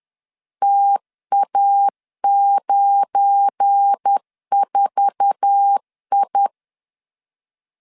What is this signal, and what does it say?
Morse code, heard as a keyed tone, about 10 wpm, 800 Hz. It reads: TA94I
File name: LEZIONI / 8 YJ45 / 8 YJ4 10wpm 2633